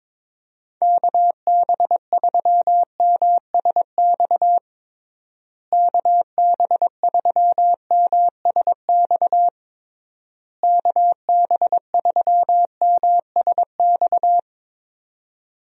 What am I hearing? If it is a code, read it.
Morse code, heard as a keyed tone, about 22 wpm, 700 Hz. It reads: KB3MSX KB3MSX KB3MSX
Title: USA Call Signs - 6 Characters - Repeated 3x - 22wpm